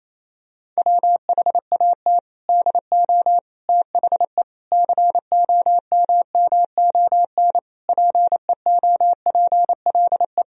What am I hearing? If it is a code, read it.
Morse code, heard as a keyed tone, about 28 wpm, 700 Hz. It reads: WHAT DO THE COMMON PEOPLE
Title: Sentences from Top 1000 Words - Review Part 3 - 28wpm